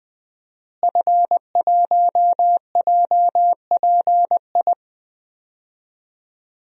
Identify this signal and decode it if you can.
Morse code, heard as a keyed tone, about 20 wpm, 700 Hz. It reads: F1JPI
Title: DXCC-100 Call Signs - 5 Characters - Repeated 2x - 20wpm